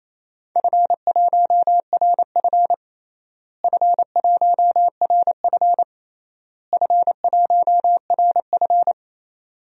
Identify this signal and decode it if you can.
Morse code, heard as a keyed tone, about 27 wpm, 700 Hz. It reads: F1RF F1RF F1RF
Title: DXCC-100 Call Signs - 4 Characters - Repeated 3x - 28wpm